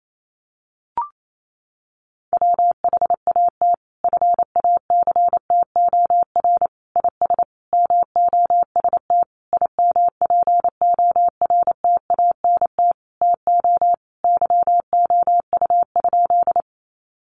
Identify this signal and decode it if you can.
Morse code, heard as a keyed tone, about 28 wpm, 700 Hz. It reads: WHAT FACTOR IS MOST IMPORTANT TO YOU?
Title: Sentences from Top 1300 Words 28wpm